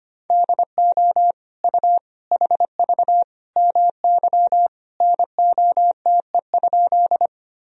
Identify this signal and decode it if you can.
Morse code, heard as a keyed tone, about 25 wpm, 700 Hz. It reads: DO U HV MY NOTE?
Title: Sentences from Top 500 Words and Abbreviations 25wpm